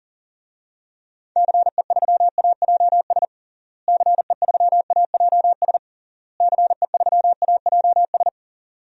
Morse code, heard as a keyed tone, about 40 wpm, 700 Hz. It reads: CE3AJS CE3AJS CE3AJS
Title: DXCC-100 Call Signs - 6 Characters - Repeated 3x - 40wpm